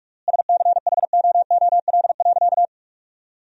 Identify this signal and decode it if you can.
Morse code, heard as a keyed tone, about 46 wpm, 700 Hz. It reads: SCHOOL.
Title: 3x Word-Spacing - Sentences from Top 300 Words - 45wpm